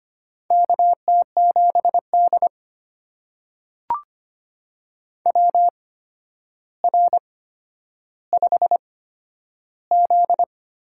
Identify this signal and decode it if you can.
Morse code, heard as a keyed tone, about 25 wpm, spaced as for 11 wpm, 700 Hz. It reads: KT7D WR5Z
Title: USA Call Signs - 4 Characters - 8x Character Spacing - 25wpm